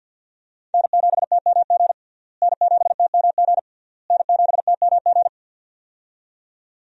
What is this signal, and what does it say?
Morse code, heard as a keyed tone, about 53 wpm, 700 Hz. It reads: N7TMG N7TMG N7TMG
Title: USA Call Signs - 5 Characters - Repeated 3x - 50wpm